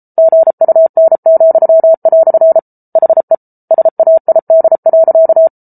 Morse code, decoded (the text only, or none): GUN," HE SAID.